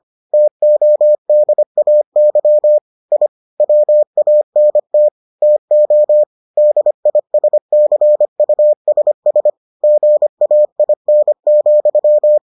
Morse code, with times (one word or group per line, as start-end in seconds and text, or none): 0.3-2.8 TODAY
3.1-3.3 I
3.6-5.1 WANT
5.4-6.2 TO
6.6-9.5 DISCUSS
9.8-12.4 GAIN,